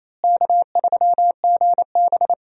Morse code, heard as a keyed tone, about 28 wpm, 700 Hz. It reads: K3GB
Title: USA Call Signs - 4 Characters - Repeated 3x - 28wpm